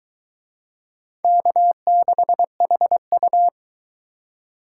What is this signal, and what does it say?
Morse code, heard as a keyed tone, about 22 wpm, 700 Hz. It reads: K6HU